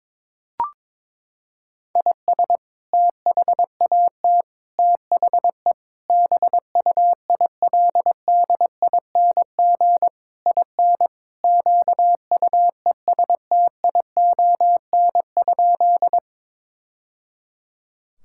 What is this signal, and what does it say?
Morse code, heard as a keyed tone, about 22 wpm, 700 Hz. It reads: IS THAT THE BUILDING IN QUESTION?
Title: Sentences from Top 1000 Words - Review Part 5 - 22wpm